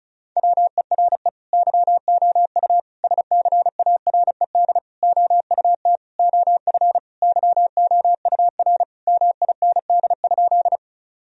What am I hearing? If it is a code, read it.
Morse code, heard as a keyed tone, about 36 wpm, 700 Hz. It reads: WERE YOU SCARED OUT OF YOUR MIND?
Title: Sentences from Top 1000 Words - Review Part 4 - 35wpm